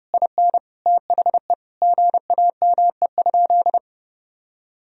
Morse code, heard as a keyed tone, about 32 wpm, 700 Hz. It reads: IN THE GAME?
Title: Sentences from Top 1000 Words - Review Part 4 - 30wpm